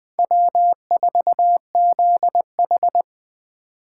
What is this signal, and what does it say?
Morse code, heard as a keyed tone, about 20 wpm, 700 Hz. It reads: W4ZH
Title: Speed Racing - USA 1x2 Call Signs 17wpm